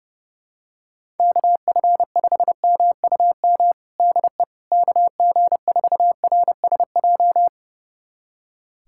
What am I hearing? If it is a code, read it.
Morse code, heard as a keyed tone, about 30 wpm, 700 Hz. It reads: KF5MUM DE KG4RSJ
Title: USA Callsign DE Callsign - 6 Characters - 30wpm